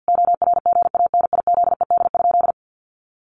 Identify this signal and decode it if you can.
Morse code, heard as a keyed tone, about 50 wpm, 700 Hz. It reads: ORGANIZED?
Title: Sentences from Top 1300 Words 50wpm